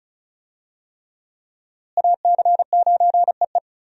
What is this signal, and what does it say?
Morse code, heard as a keyed tone, about 34 wpm, 700 Hz. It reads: AC9EE
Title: USA Call Signs - 5 Characters - Repeated 3x - 35wpm